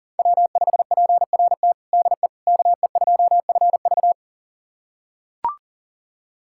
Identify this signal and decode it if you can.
Morse code, heard as a keyed tone, about 40 wpm, 700 Hz. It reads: W5PRT DE KE2FV
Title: USA Callsign DE Callsign - 5 Characters - 40wpm